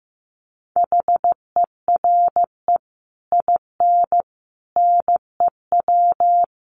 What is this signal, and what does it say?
Morse code, heard as a keyed tone, about 15 wpm, 700 Hz. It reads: HERE IN NEW
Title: Sets of 5 Words Encore2 15wpm